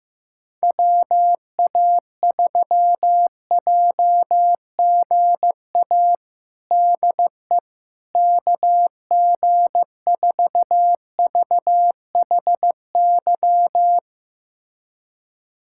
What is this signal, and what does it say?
Morse code, heard as a keyed tone, about 15 wpm, 700 Hz. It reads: WA3JGA DE KG4VHY
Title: USA Callsign DE Callsign 15wpm